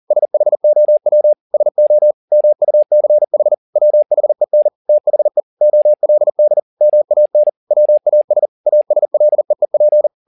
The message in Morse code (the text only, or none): SHOW SO MUCH WHEN THE OLD MAN WAS ASLEEP